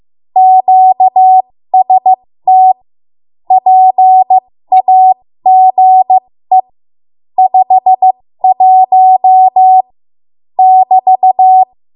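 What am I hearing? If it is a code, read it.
Morse code, heard as a keyed tone, about 15 wpm, spaced as for 13 wpm, 750 Hz. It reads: QST PAGE 51 <BT>
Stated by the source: 13 wpm Morse code practice archive transmission.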